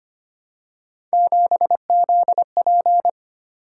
Morse code, heard as a keyed tone, about 25 wpm, 700 Hz. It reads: 7ZP
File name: Three Letter-Number Combinations - 25wpm